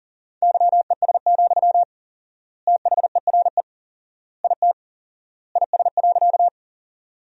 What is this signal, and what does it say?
Morse code, heard as a keyed tone, about 40 wpm, 700 Hz. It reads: YES, THERE IT IS.
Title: 4x Word-Spacing - Sentences from Top 500 Words - 40wpm